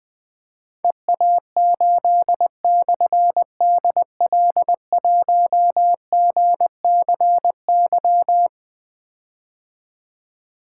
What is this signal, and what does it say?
Morse code, heard as a keyed tone, about 21 wpm, 700 Hz. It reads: EA8/DL1GCY
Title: Difficult Call Signs 20wpm